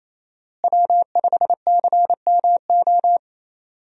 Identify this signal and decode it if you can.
Morse code, heard as a keyed tone, about 28 wpm, 700 Hz. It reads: W5CMO
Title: USA Call Signs - 5 Characters - Repeated 2x - 28wpm